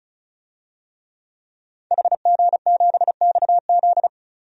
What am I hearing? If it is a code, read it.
Morse code, heard as a keyed tone, about 36 wpm, 700 Hz. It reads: HG7XZ
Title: DXCC-100 Call Signs - 5 Characters - 35wpm